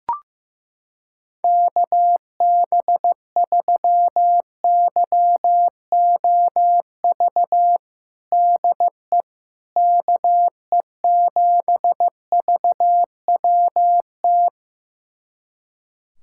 Morse code, heard as a keyed tone, about 15 wpm, 700 Hz. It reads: KB3YOV DE KE7VWT
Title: USA Callsign DE Callsign - 6 Characters - 15wpm